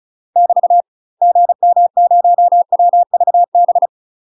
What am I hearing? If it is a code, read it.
Morse code, heard as a keyed tone, about 36 wpm, 700 Hz. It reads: <BT> GM0WVB
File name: UK MASTER QSOs TRAINING / 35WPM / UK MASTER QSOs 0035